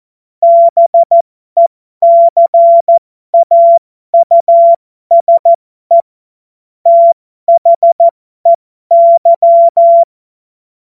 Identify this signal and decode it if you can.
Morse code, heard as a keyed tone, about 14 wpm, spaced as for 12 wpm, 680 Hz. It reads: BECAUSE THEY
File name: Morsekurs / Witze